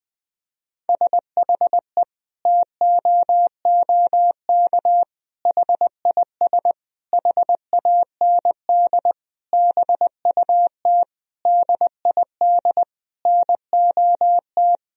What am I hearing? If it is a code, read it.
Morse code, heard as a keyed tone, about 20 wpm, 700 Hz. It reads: SHE TOOK HIS HAND BUT DID NOT
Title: Sentences from Top 1000 Words - Review Part 1 - 20wpm